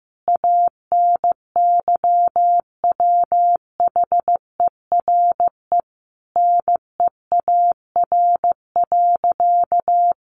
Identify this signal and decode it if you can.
Morse code, heard as a keyed tone, about 15 wpm, 700 Hz. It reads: ANYWHERE NEAR.
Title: Sentences from Top 1500 Words 15wpm